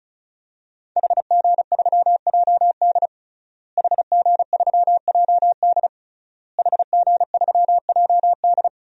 Morse code, heard as a keyed tone, about 36 wpm, 700 Hz. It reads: HG3JD HG3JD HG3JD
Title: DXCC-100 Call Signs - 5 Characters - Repeated 3x - 35wpm